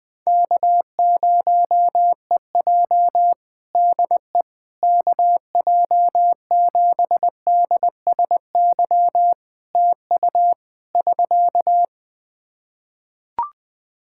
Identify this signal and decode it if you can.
Morse code, heard as a keyed tone, about 21 wpm, 700 Hz. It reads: K0EJ DE KJ7DSY TU <SK>
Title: Practice QSO - Exchange 4 - Ending - 20wpm